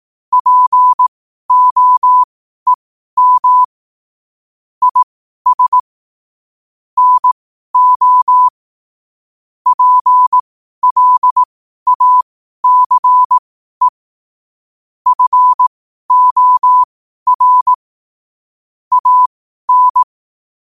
Morse code, heard as a keyed tone, about 18 wpm, spaced as for 12 wpm, 1000 Hz. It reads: POEM IS NO PLACE FOR AN
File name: qotdfiles / 12wpm